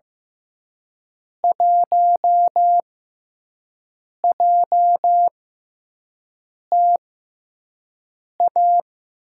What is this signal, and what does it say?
Morse code, heard as a keyed tone, about 15 wpm, spaced as for 5 wpm, 700 Hz. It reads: 1JTA
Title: USA Call Signs - 6x Character Spacing - 15wpm